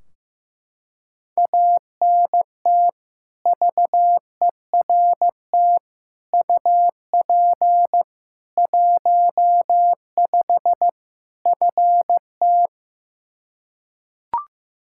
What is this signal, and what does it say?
Morse code, heard as a keyed tone, about 16 wpm, 700 Hz. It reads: ANT VERT UP 15 FT
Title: Practice QSO - Exchange 2 - Info - 15wpm